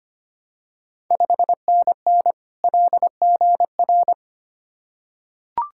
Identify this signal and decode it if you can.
Morse code, heard as a keyed tone, about 25 wpm, 690 Hz. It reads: 5NN LGR
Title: Ontario QSO Party Exchange 25wpm